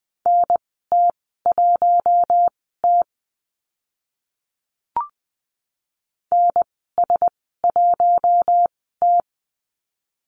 Morse code, heard as a keyed tone, about 20 wpm, spaced as for 15 wpm, 700 Hz. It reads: NT1T NS1T
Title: 048 Lesson 9g - Number 4 - Call Signs 20wpm